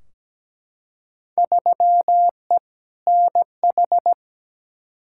Morse code, heard as a keyed tone, about 17 wpm, 700 Hz. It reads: 3E NH
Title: Field Day Exchanges Encore - 17wpm